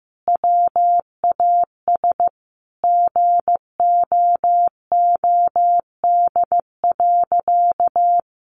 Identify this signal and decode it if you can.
Morse code, heard as a keyed tone, about 15 wpm, 700 Hz. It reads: WAS GOOD.
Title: Sentences from Top 1300 Words 15wpm